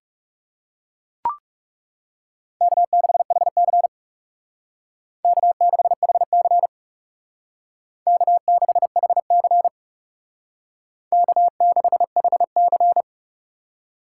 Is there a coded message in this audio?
Yes: Morse code from 2.6 to 13.0 s, heard as a keyed tone, about 38 wpm, 700 Hz. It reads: K6HC K6HC K6HC K6HC